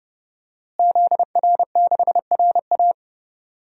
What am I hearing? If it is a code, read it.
Morse code, heard as a keyed tone, about 30 wpm, 700 Hz. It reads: ZR6RA
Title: DXCC-100 Call Signs - 5 Characters - Repeated 3x - 30wpm